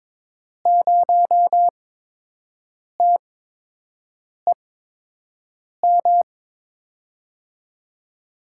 Morse code, heard as a keyed tone, about 22 wpm, spaced as for 6 wpm, 700 Hz. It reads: 0TEM
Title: USA Call Signs - 5 Characters - 8x Character Spacing - 22wpm